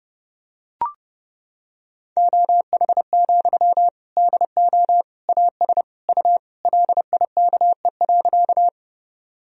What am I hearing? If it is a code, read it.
Morse code, heard as a keyed tone, about 30 wpm, 700 Hz. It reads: OH, DO AS U LIKE.
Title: Sentences from Top 500 Words and Abbreviations 30wpm